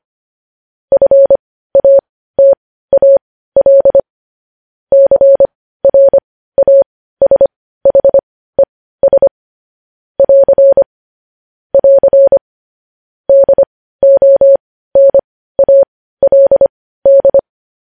Morse code, heard as a keyed tone, about 25 wpm, spaced as for 15 wpm, 550 Hz. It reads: FATAL CRASHES <AR> <AR> DONALD
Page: news-headlines-in-morse-code-at-15-wpm